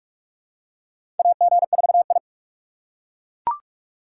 Morse code, heard as a keyed tone, about 41 wpm, 700 Hz. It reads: AG4I
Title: USA Call Signs - 4 Characters - 4x Character Spacing - 45wpm